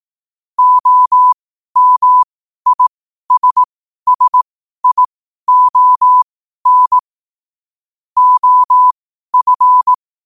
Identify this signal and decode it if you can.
Morse code, heard as a keyed tone, about 18 wpm, spaced as for 12 wpm, 1000 Hz. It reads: OMISSION OF
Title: 12 WPM morse code quotes for Thu, 29 Feb 2024